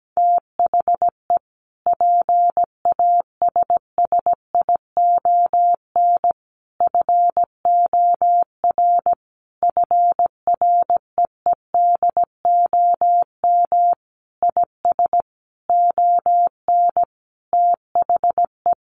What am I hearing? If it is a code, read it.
Morse code, heard as a keyed tone, about 17 wpm, 700 Hz. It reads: THE PASSION FOR FREEDOM IS ON THE
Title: Sentences from Top 1300 Words 17wpm